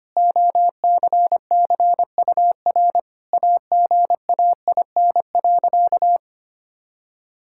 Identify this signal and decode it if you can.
Morse code, heard as a keyed tone, about 25 wpm, 700 Hz. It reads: OCCUR AGAIN.
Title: Sentences from Top 1000 Words - Review Part 6 - 25wpm